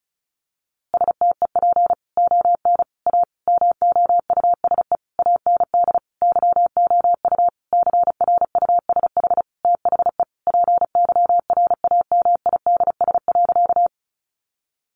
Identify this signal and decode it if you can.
Morse code, heard as a keyed tone, about 34 wpm, 700 Hz. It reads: STEP ON A MOUSE AND YOU CRUSH THE PYRAMIDS.
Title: A Sound Of Thunder 35wpm